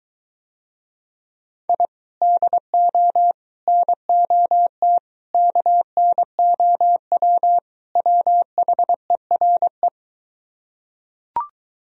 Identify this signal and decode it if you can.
Morse code, heard as a keyed tone, about 23 wpm, 700 Hz. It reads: I DO NOT KNOW WHERE